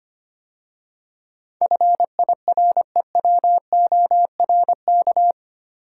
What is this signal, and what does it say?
Morse code, heard as a keyed tone, about 25 wpm, 700 Hz. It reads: FIREWORK
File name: 8 Letter Words - First 3 Letters Given - 25wpm